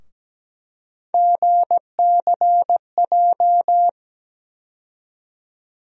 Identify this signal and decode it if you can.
Morse code, heard as a keyed tone, about 17 wpm, 700 Hz. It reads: GCJ
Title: USA Call Sign Postfixes - 3 Letters 17wpm